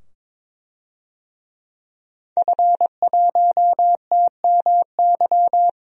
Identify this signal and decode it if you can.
Morse code, heard as a keyed tone, about 22 wpm, 700 Hz. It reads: F1TMY
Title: DXCC-100 Call Signs - 5 Characters - 22wpm